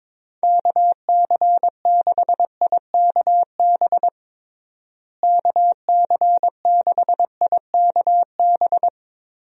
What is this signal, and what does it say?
Morse code, heard as a keyed tone, about 22 wpm, 700 Hz. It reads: KC6IKB KC6IKB
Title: USA Call Signs - 6 Characters - Repeated 3x - 22wpm